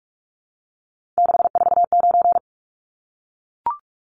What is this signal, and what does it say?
Morse code, heard as a keyed tone, about 44 wpm, 700 Hz. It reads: 649
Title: Numbers - 3-Digit 45wpm